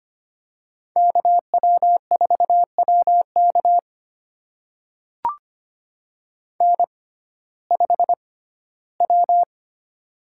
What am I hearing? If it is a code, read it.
Morse code, heard as a keyed tone, about 25 wpm, spaced as for 15 wpm, 700 Hz. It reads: KW4WK N5W
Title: USA Call Signs - 6x Character Spacing - 25wpm